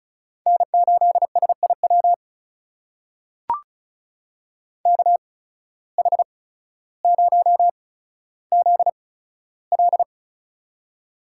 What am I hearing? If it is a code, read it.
Morse code, heard as a keyed tone, about 36 wpm, spaced as for 15 wpm, 700 Hz. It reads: N8SIW KH0ZL
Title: USA Call Signs - 5 Characters - 8x Character Spacing - 35wpm